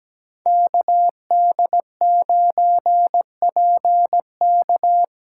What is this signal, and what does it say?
Morse code, heard as a keyed tone, about 17 wpm, 700 Hz. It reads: KD9PK
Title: USA Call Signs - 5 Characters - 3x Character Spacing - 17wpm